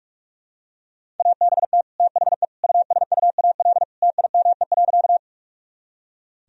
Morse code, heard as a keyed tone, about 46 wpm, 700 Hz. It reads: ABT THE USUAL TIME.
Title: Sentences from Top 400 Words and Abbreviations 45wpm